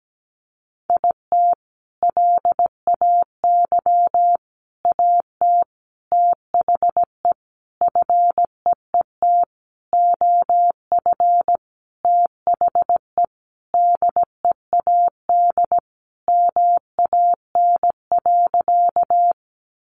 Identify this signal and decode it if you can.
Morse code, heard as a keyed tone, about 17 wpm, 700 Hz. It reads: IT LAY AT THE FEET OF THE DEAD MAN.
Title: Sentences from Top 1500 Words 17wpm